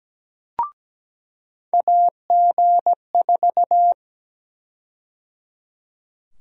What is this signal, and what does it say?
Morse code, heard as a keyed tone, about 17 wpm, 700 Hz. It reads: AG4
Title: USA Call Sign Prefixes - 17wpm